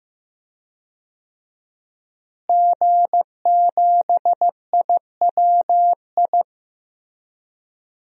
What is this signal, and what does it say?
Morse code, heard as a keyed tone, about 15 wpm, 700 Hz. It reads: G7IWI
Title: DXCC-100 Call Signs - 5 Characters - Repeated 3x - 15wpm